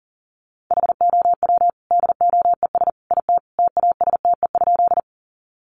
Morse code, heard as a keyed tone, about 40 wpm, 700 Hz. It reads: HOW DOES IT TASTE?
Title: Sentences from Top 1100 Words 40wpm